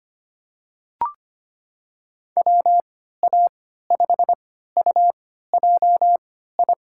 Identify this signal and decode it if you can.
Morse code, heard as a keyed tone, about 25 wpm, spaced as for 14 wpm, 700 Hz. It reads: WA5UJI
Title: USA Call Signs - 6 Characters - 3x Character Spacing - 25wpm